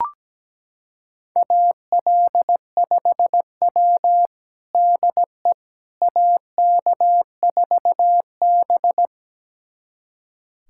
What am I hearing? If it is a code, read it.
Morse code, heard as a keyed tone, about 17 wpm, 700 Hz. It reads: AL5W DE AK4B
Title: USA Callsign DE Callsign - 4 Characters - 17wpm